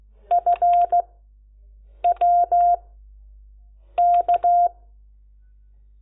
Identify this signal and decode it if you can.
Morse code, heard as a keyed tone, about 16 wpm, spaced as for 6 wpm, 700 Hz. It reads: FWK
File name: Lecciones / CLASE Nº10